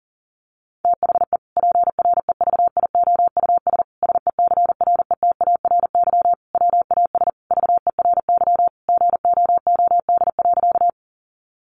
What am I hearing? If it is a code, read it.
Morse code, heard as a keyed tone, about 40 wpm, 700 Hz. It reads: THE PREVIOUS SECRETARY WAS VERY GOOD.
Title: Sentences from Top 1500 Words 40wpm